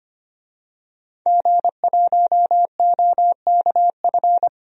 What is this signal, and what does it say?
Morse code, heard as a keyed tone, about 25 wpm, 700 Hz. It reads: G1OKF
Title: DXCC-100 Call Signs - 5 Characters - 25wpm